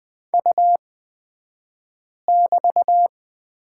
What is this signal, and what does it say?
Morse code, heard as a keyed tone, about 20 wpm, spaced as for 5 wpm, 700 Hz. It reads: U<BT>
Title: 249 Lesson 40d - BK - Two Characters 20wpm